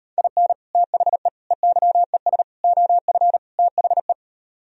Morse code, heard as a keyed tone, about 39 wpm, 700 Hz. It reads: IN THE EYES OF THE